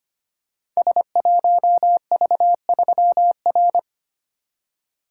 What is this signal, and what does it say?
Morse code, heard as a keyed tone, about 25 wpm, 700 Hz. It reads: S1V3R
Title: Code Groups - 5 Letter:Numbers - 25wpm